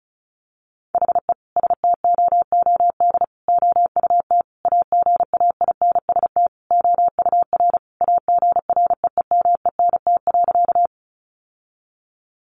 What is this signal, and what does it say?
Morse code, heard as a keyed tone, about 34 wpm, 700 Hz. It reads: HE STOOD OUT AGAINST OUR AGREEMENT.
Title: Sentences from Top 1500 Words 35wpm